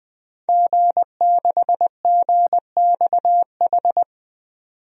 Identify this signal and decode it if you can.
Morse code, heard as a keyed tone, about 21 wpm, 700 Hz. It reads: G6GXH
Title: DXCC-100 Call Signs - 5 Characters - 20wpm